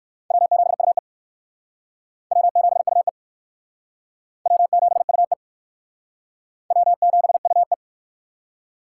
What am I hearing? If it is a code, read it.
Morse code, heard as a keyed tone, about 59 wpm, 700 Hz. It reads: W7UE W7UE W7UE W7UE